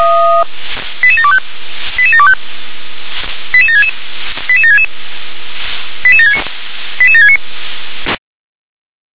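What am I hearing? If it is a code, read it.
Five-tone selective call (ZVEI), heard as quick runs of five tones, tones of about 80 ms. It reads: 88725 88725 88766 88766 88769 88769
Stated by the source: ZVEI five-tone selective call 88766, received on an SDR.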